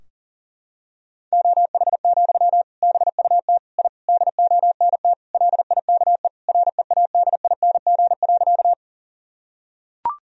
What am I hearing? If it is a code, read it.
Morse code, heard as a keyed tone, about 40 wpm, 700 Hz. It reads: OH, BUT I DONT LIKE READING.
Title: Sentences from Top 1000 Words - Review Part 3 - 40wpm